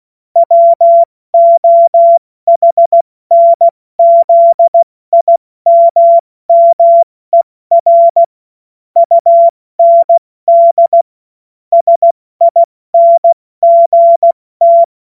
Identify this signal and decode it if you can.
Morse code, heard as a keyed tone, about 16 wpm, spaced as for 14 wpm, 680 Hz. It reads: WOHNZIMMER UND SINGT
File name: Morsekurs / Witze / Witz 24. Januar 2023 70 F80